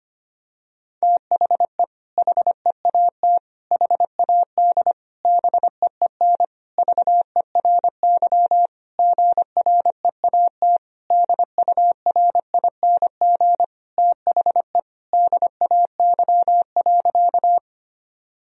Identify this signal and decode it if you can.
Morse code, heard as a keyed tone, about 25 wpm, 700 Hz. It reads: THE HEAT HAD BEEN VERY GREAT DURING THE DAY.
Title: Sentences from Top 1000 Words - Review Part 3 - 25wpm